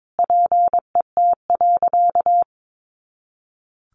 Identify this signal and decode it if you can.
Morse code, heard as a keyed tone, about 22 wpm, 700 Hz. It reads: PET.
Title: Additional Easy Phrases 22wpm